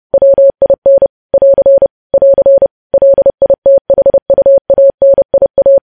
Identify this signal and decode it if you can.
Morse code, heard as a keyed tone, about 30 wpm, 550 Hz. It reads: WIN <AR> <AR> LITHUANIA